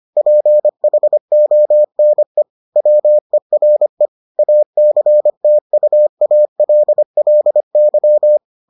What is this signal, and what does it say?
Morse code, heard as a keyed tone, about 25 wpm, 600 Hz. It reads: PHONE WERE ACTUALLY